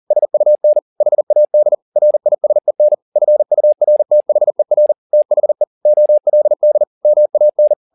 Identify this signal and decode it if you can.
Morse code, heard as a keyed tone, about 40 wpm, 600 Hz. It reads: SUN HAD RISEN FURTHER THE OLD MAN